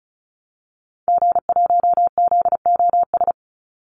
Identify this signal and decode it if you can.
Morse code, heard as a keyed tone, about 34 wpm, 700 Hz. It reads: G1ZOS
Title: Call Signs - 35wpm